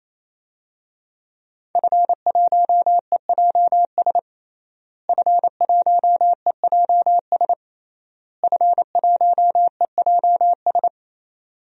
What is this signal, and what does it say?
Morse code, heard as a keyed tone, about 28 wpm, 700 Hz. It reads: F1EJS F1EJS F1EJS
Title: DXCC-100 Call Signs - 5 Characters - Repeated 3x - 28wpm